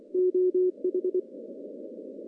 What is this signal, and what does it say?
Morse code, heard as a keyed tone, about 24 wpm, 370 Hz. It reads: OH
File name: contests / CW-SS-k5tr- -1500